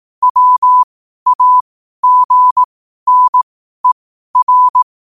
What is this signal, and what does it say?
Morse code, heard as a keyed tone, about 18 wpm, spaced as for 12 wpm, 1000 Hz. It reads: WAGNER
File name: qotdfiles / 12wpm